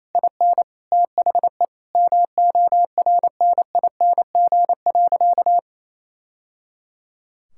Morse code, heard as a keyed tone, about 28 wpm, 700 Hz. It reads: IN THE MORNING.
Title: Sentences from Top 1000 Words - Review Part 6 - 28wpm